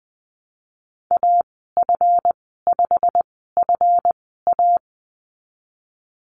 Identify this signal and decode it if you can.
Morse code, heard as a keyed tone, about 19 wpm, spaced as for 14 wpm, 700 Hz. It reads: AF5FA
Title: 140 Lesson 23g - Letter f - Call Signs 20wpm